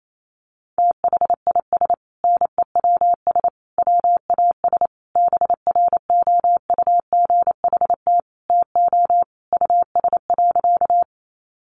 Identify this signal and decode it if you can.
Morse code, heard as a keyed tone, about 28 wpm, 700 Hz. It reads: THIS NEWS WAS BROUGHT TO US.
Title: Sentences from Top 800 Words 28wpm